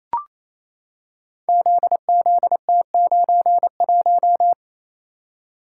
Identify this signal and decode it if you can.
Morse code, heard as a keyed tone, about 27 wpm, 700 Hz. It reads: ZZT91
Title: Code Groups - 5 Letter:Numbers - 28wpm